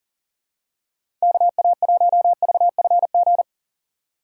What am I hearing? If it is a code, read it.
Morse code, heard as a keyed tone, about 40 wpm, 690 Hz. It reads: KA1VFG
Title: USA Call Signs - 6 Characters - 2x Character Spacing - 40wpm